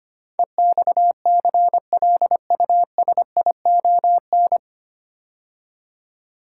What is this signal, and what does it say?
Morse code, heard as a keyed tone, about 25 wpm, 700 Hz. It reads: EXCLUSION
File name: practice / 25wpm / 9 Letter Words - First 3 Letters Given - 25wpm